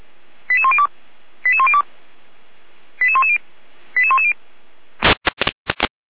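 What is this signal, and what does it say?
Five-tone selective call (ZVEI), heard as quick runs of five tones, tones of about 70 ms. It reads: 88102 88102 88110 88110
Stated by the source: ZVEI five-tone call received by SDR, from feuerwehr-orlamuende.org fire brigade audio server.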